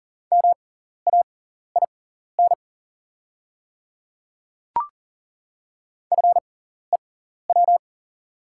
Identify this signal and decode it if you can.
Morse code, heard as a keyed tone, about 40 wpm, spaced as for 14 wpm, 690 Hz. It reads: MAIN FEW
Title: 6x Character Spacing - Top 500 Words - 40wpm